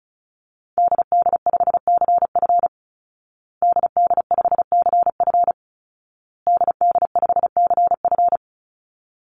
Morse code, heard as a keyed tone, about 34 wpm, 700 Hz. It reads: DD5CF DD5CF DD5CF
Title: Call Signs Repeated 2x - 35wpm